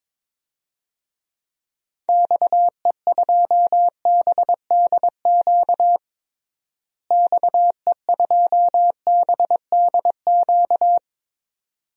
Morse code, heard as a keyed tone, about 22 wpm, 700 Hz. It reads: XE2BDQ XE2BDQ
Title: DXCC-100 Call Signs - 6 Characters - Repeated 2x - 22wpm